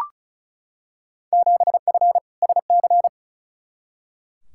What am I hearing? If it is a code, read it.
Morse code, heard as a keyed tone, about 35 wpm, 700 Hz. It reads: 7F SC
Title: Field Day Exchanges Encore - 35wpm